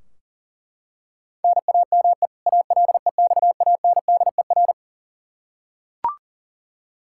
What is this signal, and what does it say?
Morse code, heard as a keyed tone, about 40 wpm, 700 Hz. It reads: NAME ALEXANDER
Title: Practice QSO - Exchange 1 - Beginning - 40wpm